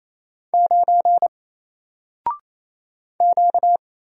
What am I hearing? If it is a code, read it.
Morse code, heard as a keyed tone, about 27 wpm, 700 Hz. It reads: 9 Q
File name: Single-Letter-Number-28wpm